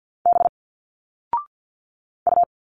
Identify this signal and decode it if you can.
Morse code, heard as a keyed tone, about 50 wpm, 700 Hz. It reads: B U
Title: Single Letters 50wpm